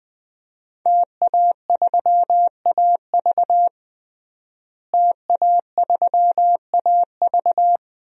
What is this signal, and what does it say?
Morse code, heard as a keyed tone, about 20 wpm, 700 Hz. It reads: TA3AV TA3AV
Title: DXCC-100 Call Signs - 5 Characters - Repeated 2x - 20wpm